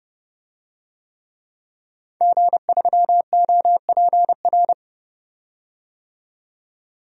Morse code, heard as a keyed tone, about 32 wpm, 700 Hz. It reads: G3OPR
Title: DXCC-100 Call Signs - 5 Characters - 30wpm